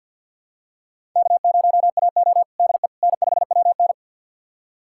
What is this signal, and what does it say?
Morse code, heard as a keyed tone, about 48 wpm, 700 Hz. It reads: K0AO DE N5WN
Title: USA Callsign DE Callsign - 4 Characters - 50wpm